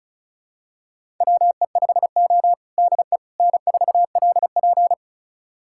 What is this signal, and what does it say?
Morse code, heard as a keyed tone, about 36 wpm, 700 Hz. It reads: WE5O DE N4LP
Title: USA Callsign DE Callsign - 4 Characters - 35wpm